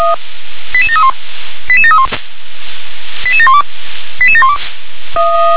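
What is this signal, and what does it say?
Five-tone selective call (ZVEI), heard as quick runs of five tones, tones of about 70 ms. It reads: 88731 88731 88712 88712
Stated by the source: ZVEI five-tone selective call 88709, received by SDR.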